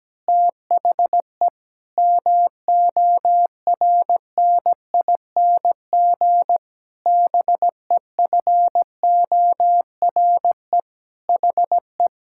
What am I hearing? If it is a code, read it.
Morse code, heard as a keyed tone, about 17 wpm, 700 Hz. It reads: THE MORNING BEFORE HE
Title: Pangram Challenge v2 - 17wpm